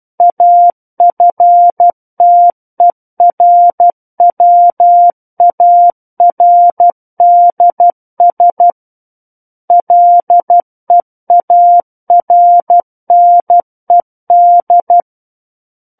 Morse code, heard as a keyed tone, about 12 wpm, 700 Hz. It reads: AFTERWARDS LEARNED
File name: Book One / 03-Chapter 3 - 12 WPM